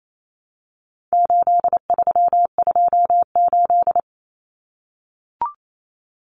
Morse code, heard as a keyed tone, about 28 wpm, 700 Hz. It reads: 8328
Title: Numbers - 4-Digit 28wpm